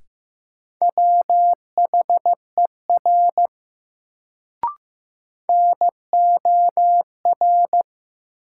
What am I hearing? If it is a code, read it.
Morse code, heard as a keyed tone, about 15 wpm, 700 Hz. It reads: WHER NOR
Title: Sentences from Top 1500 Words - New Word Drill - 15wpm